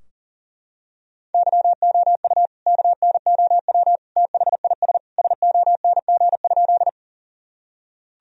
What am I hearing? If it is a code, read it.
Morse code, heard as a keyed tone, about 37 wpm, 700 Hz. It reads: YOU KNOW THIS SONG?
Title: Sentences from Top 1000 Words - Review Part 2 - 40wpm